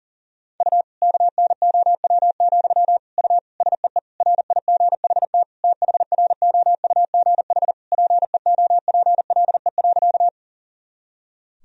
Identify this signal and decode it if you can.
Morse code, heard as a keyed tone, about 37 wpm, 700 Hz. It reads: U KNOW, U SEE RIGHT THROUGH PEOPLE.
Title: Sentences from Top 200 Words and Abbreviations 40wpm